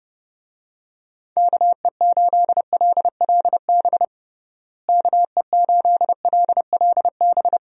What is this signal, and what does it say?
Morse code, heard as a keyed tone, about 30 wpm, 700 Hz. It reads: KE8LLB KE8LLB
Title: USA Call Signs - 6 Characters - Repeated 2x - 30wpm